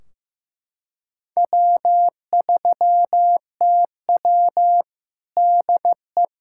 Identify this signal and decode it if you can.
Morse code, heard as a keyed tone, about 15 wpm, 700 Hz. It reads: W3TW DE
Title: Practice QSO - Exchange 4 - Ending - 15wpm